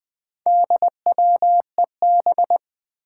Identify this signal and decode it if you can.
Morse code, heard as a keyed tone, about 20 wpm, 700 Hz. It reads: DWEB
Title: Four Challenging Letter-Number Combinations 20wpm